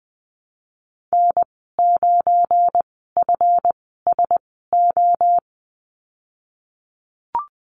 Morse code, heard as a keyed tone, about 20 wpm, spaced as for 15 wpm, 700 Hz. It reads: N9FSO
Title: 173 Lesson 28g - Number 9 - Call Signs 20wpm